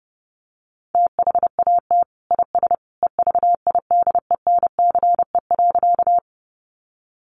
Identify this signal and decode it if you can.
Morse code, heard as a keyed tone, about 30 wpm, 700 Hz. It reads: THAT IS EVIDENCE.
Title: Sentences from Top 1300 Words 30wpm